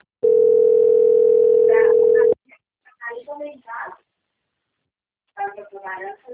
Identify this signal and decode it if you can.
Ringback tone (North America), 440 + 480 Hz, with speech.
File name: out- - - - - .27216